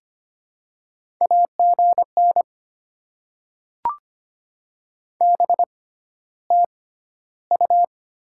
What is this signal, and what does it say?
Morse code, heard as a keyed tone, about 25 wpm, spaced as for 13 wpm, 700 Hz. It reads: AGN BTU
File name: Top 20 QSO Elements - 6x Character Spacing - 25wpm